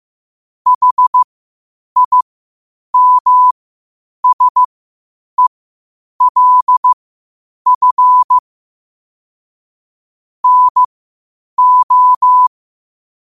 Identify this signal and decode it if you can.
Morse code, heard as a keyed tone, about 15 wpm, spaced as for 8 wpm, 1000 Hz. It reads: HIMSELF NO
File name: qotdfiles / 8wpm